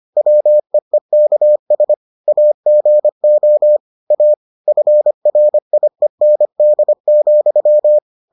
Morse code, heard as a keyed tone, about 25 wpm, 600 Hz. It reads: WEEKS AGO A FRIEND,